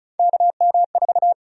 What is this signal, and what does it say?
Morse code, heard as a keyed tone, about 36 wpm, 700 Hz. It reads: KM4
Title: USA Call Sign Prefixes - 35wpm